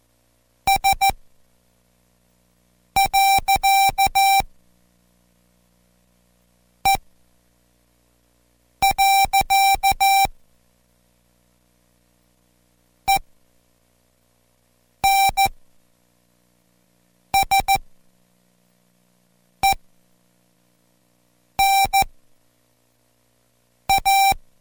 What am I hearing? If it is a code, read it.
Morse code, heard as a keyed tone, about 14 wpm, spaced as for 4 wpm, 780 Hz. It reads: S.E. ENSENA